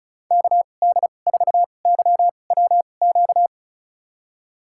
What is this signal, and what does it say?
Morse code, heard as a keyed tone, about 36 wpm, spaced as for 25 wpm, 690 Hz. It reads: KD4YWQ
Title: USA Call Signs - 6 Characters - 2x Character Spacing - 35wpm